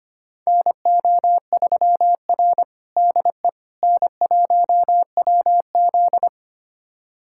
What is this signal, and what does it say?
Morse code, heard as a keyed tone, about 25 wpm, 700 Hz. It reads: NO3R DE N1WZ
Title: USA Callsign DE Callsign - 4 Characters - 25wpm